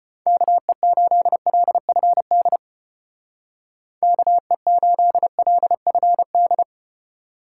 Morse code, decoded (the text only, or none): KE8LFD KE8LFD